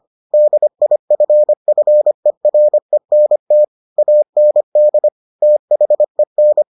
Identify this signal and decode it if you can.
Morse code, heard as a keyed tone, about 25 wpm, 600 Hz. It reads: DIFFERENT AND THEN